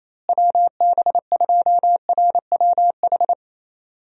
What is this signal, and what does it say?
Morse code, heard as a keyed tone, about 28 wpm, 700 Hz. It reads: WB2RWH
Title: USA Call Signs - 6 Characters - Repeated 3x - 28wpm